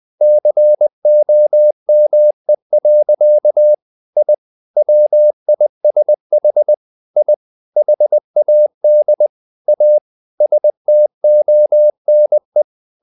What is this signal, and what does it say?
Morse code, heard as a keyed tone, about 20 wpm, 600 Hz. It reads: COME. I WISH I HAD A STONE
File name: the old man and the sea part 2 20wpm